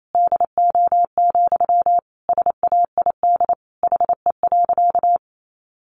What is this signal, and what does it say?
Morse code, heard as a keyed tone, about 28 wpm, 700 Hz. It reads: DO, SAID HE.
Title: Sentences from Top 100 Words 28wpm